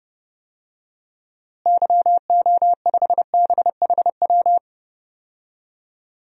Morse code, heard as a keyed tone, about 30 wpm, 700 Hz. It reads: YO5BHW
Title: DXCC-100 Call Signs - 6 Characters - Repeated 2x - 30wpm